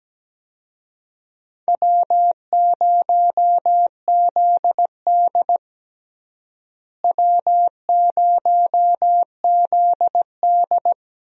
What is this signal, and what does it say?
Morse code, heard as a keyed tone, about 17 wpm, 700 Hz. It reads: W0ZD W0ZD
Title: USA Call Signs - 4 Characters - Repeated 2x - 17wpm